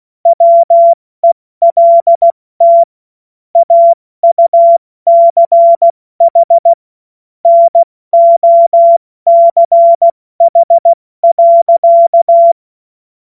Morse code, heard as a keyed tone, about 16 wpm, 680 Hz. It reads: WELT AUCH NOCH.
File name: Morsekurs / Witze / Witz 16. Februar 2021, 70 F80